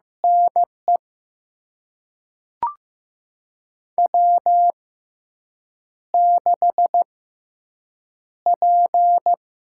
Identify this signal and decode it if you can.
Morse code, heard as a keyed tone, about 15 wpm, spaced as for 7 wpm, 700 Hz. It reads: NE W6P
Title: USA Call Signs - 6x Character Spacing - 15wpm